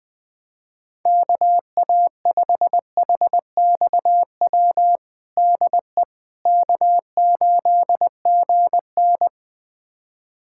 Morse code, heard as a keyed tone, about 19 wpm, 700 Hz. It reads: KA5HXW DE K8GN
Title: USA Callsign DE Callsign 20wpm